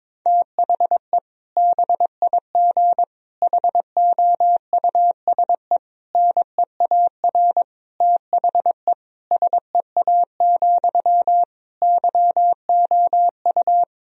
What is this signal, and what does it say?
Morse code, heard as a keyed tone, about 22 wpm, 700 Hz. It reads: THE BIG HOUSE NEAR THE SEA, YOU
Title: Sentences from Top 1000 Words - Review Part 1 - 22wpm